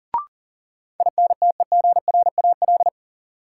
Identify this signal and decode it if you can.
Morse code, heard as a keyed tone, about 40 wpm, 700 Hz. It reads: INTEGRAL
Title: Sets of 5 Words - New Word Drill - 40wpm